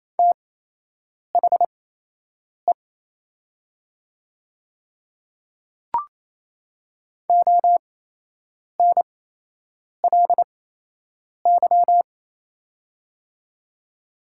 Morse code, heard as a keyed tone, about 27 wpm, spaced as for 8 wpm, 700 Hz. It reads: THE ONLY
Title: 8x Character Spacing - Top 200 Words - 28wpm